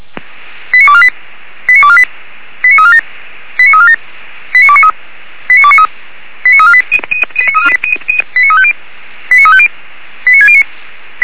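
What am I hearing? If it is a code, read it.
Five-tone selective call (ZVEI), heard as quick runs of five tones, tones of about 70 ms. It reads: 89248 89248 89347 89347 89293 89293 89348 89350 89350 89609
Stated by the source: ZVEI five-tone call 89466 received on SDR.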